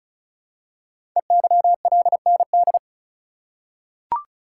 Morse code, heard as a keyed tone, about 36 wpm, 700 Hz. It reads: EYLND
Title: Code Groups - 5 Letters - 35wpm